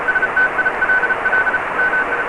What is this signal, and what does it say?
Morse code, heard as a keyed tone, about 34 wpm, 1500 Hz, with radio noise. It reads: STILL MI